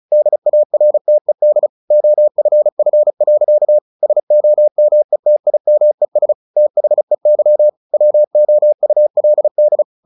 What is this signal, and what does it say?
Morse code, heard as a keyed tone, about 34 wpm, 600 Hz. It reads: DARTED OFF. SOMETIMES THEY WOULD
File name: the old man and the sea 2 35wpm